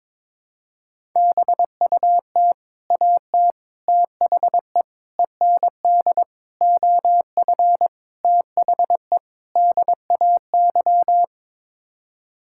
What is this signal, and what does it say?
Morse code, heard as a keyed tone, about 22 wpm, 700 Hz. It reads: BUT AT THE END OF THE DAY
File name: Sets of 7 Words - 22wpm